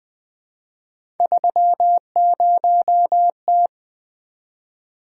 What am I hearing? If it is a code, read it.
Morse code, heard as a keyed tone, about 20 wpm, 700 Hz. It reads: 30T
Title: Three Letter-Number Combinations - 20wpm